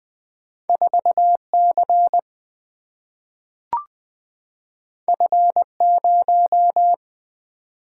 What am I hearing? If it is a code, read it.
Morse code, heard as a keyed tone, about 20 wpm, 700 Hz. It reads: 4C F0
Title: Single-Letter Single-Number 20wpm